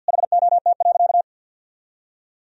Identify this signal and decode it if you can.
Morse code, heard as a keyed tone, about 48 wpm, 700 Hz. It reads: HOT.
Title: Sentences from Top 500 Words and Abbreviations 50wpm